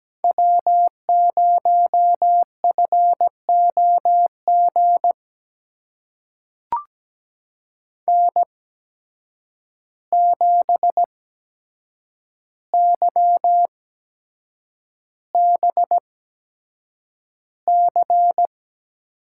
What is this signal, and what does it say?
Morse code, heard as a keyed tone, about 17 wpm, spaced as for 7 wpm, 700 Hz. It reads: W0FOG N7YBC
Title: USA Call Signs - 5 Characters - 8x Character Spacing - 17wpm